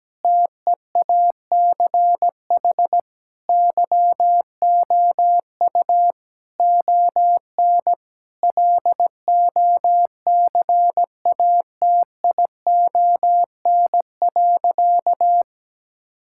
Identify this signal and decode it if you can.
Morse code, heard as a keyed tone, about 17 wpm, 700 Hz. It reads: TEACH YOU ON LOCATION.
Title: Sentences from Top 1000 Words - Review Part 6 - 17wpm